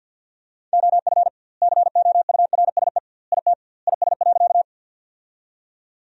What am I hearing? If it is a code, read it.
Morse code, heard as a keyed tone, about 51 wpm, 700 Hz. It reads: OF COURSE IT IS.
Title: 2x Word-Spacing - Sentences from Top 500 Words - 50wpm